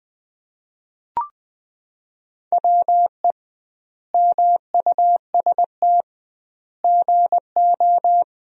Morse code, heard as a keyed tone, about 20 wpm, 700 Hz. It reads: WE MUST GO
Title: 2x Word-Spacing - Sentences from Top 300 Words - 20wpm